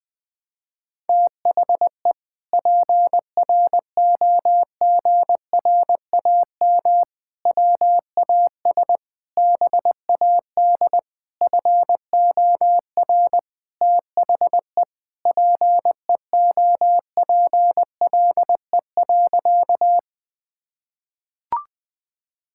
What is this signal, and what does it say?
Morse code, heard as a keyed tone, about 20 wpm, 700 Hz. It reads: THE PROGRAM WAS BAD FOR THE PEOPLE.
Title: Sentences from Top 1000 Words - Review Part 5 - 20wpm